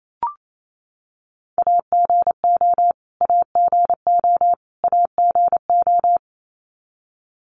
Morse code, heard as a keyed tone, about 28 wpm, 700 Hz. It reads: AGO AGO AGO
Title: Top 500 Words Repeated 28wpm